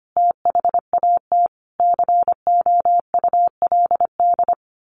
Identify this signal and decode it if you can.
Morse code, heard as a keyed tone, about 25 wpm, 700 Hz. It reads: THAT COULD
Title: A Sound Of Thunder 25wpm